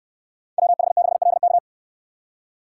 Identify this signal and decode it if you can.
Morse code, heard as a keyed tone, about 63 wpm, 690 Hz. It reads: WH6LK